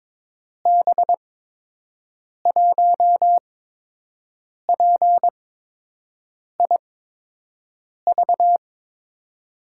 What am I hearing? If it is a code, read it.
Morse code, heard as a keyed tone, about 21 wpm, spaced as for 6 wpm, 700 Hz. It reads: B1PIV
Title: USA Call Signs - 8x Character Spacing - 22wpm